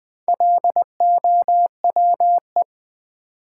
Morse code, heard as a keyed tone, about 20 wpm, 700 Hz. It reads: LOWE
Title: Build A Word - 1 Letter At A Time - 3 to 5-Letter Words - 20wpm